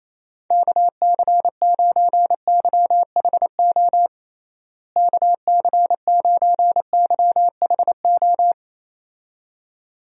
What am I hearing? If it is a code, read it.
Morse code, heard as a keyed tone, about 28 wpm, 700 Hz. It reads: KC9YHO KC9YHO
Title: USA Call Signs - 6 Characters - Repeated 2x - 28wpm